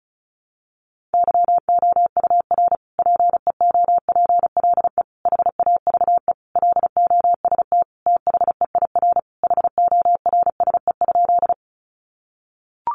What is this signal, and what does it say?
Morse code, heard as a keyed tone, about 34 wpm, 700 Hz. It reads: YOUR PEOPLE HAVE LOST THEIR HORSE?
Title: Sentences from Top 1400 Words 35wpm